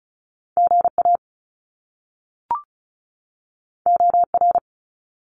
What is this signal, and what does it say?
Morse code, heard as a keyed tone, about 36 wpm, 700 Hz. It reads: GA OR
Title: Top Two Letter Combinations in Words 35wpm